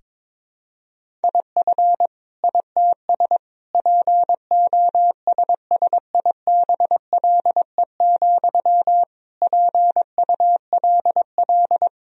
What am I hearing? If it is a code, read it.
Morse code, heard as a keyed tone, about 22 wpm, 700 Hz. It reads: IF ITS POSSIBLE, PULL
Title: Sentences from Top 1000 Words - Review Part 2 - 22wpm